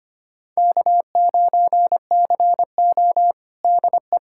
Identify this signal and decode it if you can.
Morse code, heard as a keyed tone, about 25 wpm, 700 Hz. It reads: K9CO DE
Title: USA Callsign DE Callsign - 4 Characters - 25wpm